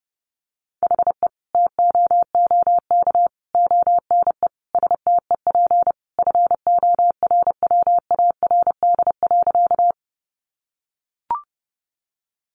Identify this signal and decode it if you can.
Morse code, heard as a keyed tone, about 30 wpm, 700 Hz. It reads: HE TOOK ONE STEP FORWARD.
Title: Sentences from Top 1300 Words 30wpm